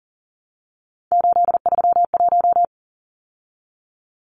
Morse code, heard as a keyed tone, about 40 wpm, 700 Hz. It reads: 831
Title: Numbers - 3-Digit 40wpm